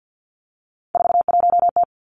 Morse code, heard as a keyed tone, about 50 wpm, 700 Hz. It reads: 41T
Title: Three Challenging Letter-Number Combinations 50wpm